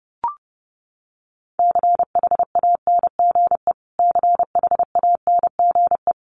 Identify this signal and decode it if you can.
Morse code, heard as a keyed tone, about 29 wpm, 700 Hz. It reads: CHANGE CHANGE
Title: Top 200 Words Repeated 30wpm